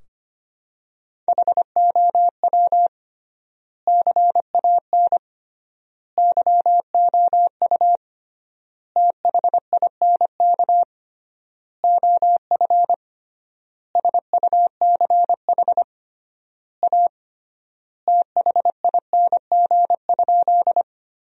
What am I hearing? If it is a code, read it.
Morse code, heard as a keyed tone, about 25 wpm, 700 Hz. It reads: HOW CAN YOU THINK OF SUCH A THING?
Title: 3x Word-Spacing - Sentences from Top 200 Words - 25wpm